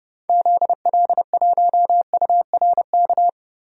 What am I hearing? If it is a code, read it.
Morse code, heard as a keyed tone, about 30 wpm, 700 Hz. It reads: ZL1URK
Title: DXCC-100 Call Signs - 6 Characters - Repeated 3x - 30wpm